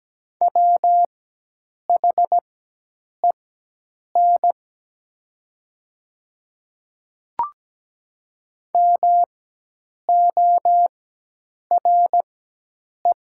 Morse code, heard as a keyed tone, about 17 wpm, spaced as for 8 wpm, 700 Hz. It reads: WHEN MORE
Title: 4x Character Spacing - Top 200 Words - 17wpm